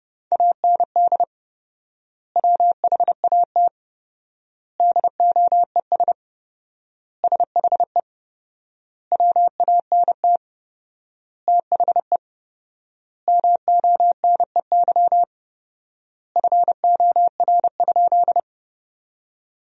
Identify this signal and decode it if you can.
Morse code, heard as a keyed tone, about 32 wpm, 700 Hz. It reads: AND WHAT DOES SHE WANT THE MONEY FOR?
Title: 4x Word-Spacing - Sentences from Top 400 Words - 30wpm